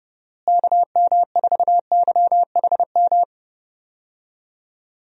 Morse code, heard as a keyed tone, about 32 wpm, 700 Hz. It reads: KM4YHM
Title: USA Call Signs - 6 Characters - Repeated 3x - 30wpm